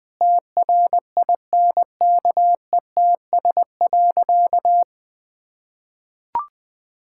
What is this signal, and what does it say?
Morse code, heard as a keyed tone, about 21 wpm, 700 Hz. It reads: TRINKETS.
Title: Sentences with 2-Word Phrases - Spoken Beginning - 20wpm